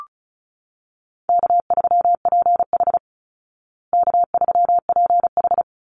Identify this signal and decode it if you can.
Morse code, heard as a keyed tone, about 34 wpm, 700 Hz. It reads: K3PH K3PH
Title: Easy Call Signs Repeated 2x - 35wpm